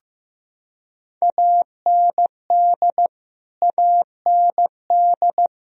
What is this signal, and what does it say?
Morse code, heard as a keyed tone, about 15 wpm, 700 Hz. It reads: AND AND
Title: Top 100 Words Repeated 15wpm